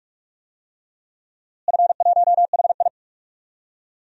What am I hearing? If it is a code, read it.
Morse code, heard as a keyed tone, about 46 wpm, 700 Hz. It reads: F1HI
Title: DXCC-100 Call Signs - 4 Characters - Repeated 3x - 45wpm